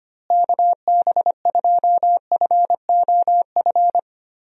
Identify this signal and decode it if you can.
Morse code, heard as a keyed tone, about 25 wpm, 700 Hz. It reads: KB2FOF
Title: USA Call Signs - 6 Characters - Repeated 2x - 25wpm